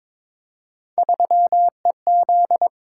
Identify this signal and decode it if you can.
Morse code, heard as a keyed tone, about 22 wpm, 700 Hz. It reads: 3EZ
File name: practice / 22wpm / Three Letter-Number Combinations - 22wpm